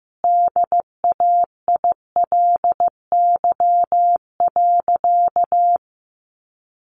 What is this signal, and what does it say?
Morse code, heard as a keyed tone, about 15 wpm, 700 Hz. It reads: DAILY.
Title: Sentences from Top 1100 Words 15wpm